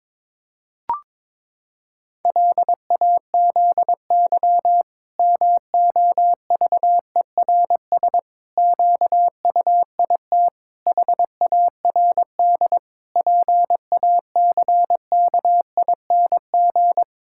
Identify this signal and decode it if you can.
Morse code, heard as a keyed tone, about 22 wpm, 700 Hz. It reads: LAZY MOVERS QUIT HARD PACKING
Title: Pangram Challenge v2 - 22wpm